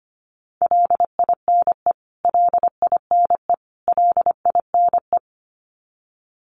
Morse code, heard as a keyed tone, about 25 wpm, 700 Hz. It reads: LINE LINE LINE
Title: Top 500 Words Repeated 25wpm